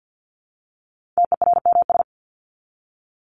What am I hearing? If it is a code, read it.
Morse code, heard as a keyed tone, about 50 wpm, 700 Hz. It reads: TERMS
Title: ES:S Suffix - Top 100 Words with Base Words 50wpm